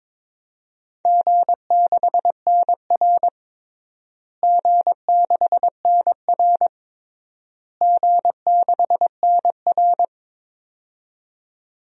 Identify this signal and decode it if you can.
Morse code, heard as a keyed tone, about 22 wpm, 700 Hz. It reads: G6NR G6NR G6NR
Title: DXCC-100 Call Signs - 4 Characters - Repeated 3x - 22wpm